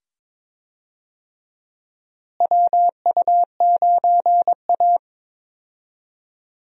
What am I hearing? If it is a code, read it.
Morse code, heard as a keyed tone, about 21 wpm, 700 Hz. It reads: WU9A
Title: USA Call Signs - 4 Characters - Repeated 2x - 22wpm